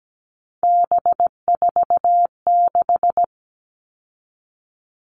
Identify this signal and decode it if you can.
Morse code, heard as a keyed tone, about 17 wpm, 700 Hz. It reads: B46
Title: All Cut Numbers - 3-Digit 17wpm